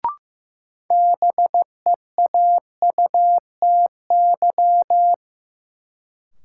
Morse code, heard as a keyed tone, about 15 wpm, 700 Hz. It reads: BEAUTY
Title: Sentences from Top 600 Words - New Word Drill - 15wpm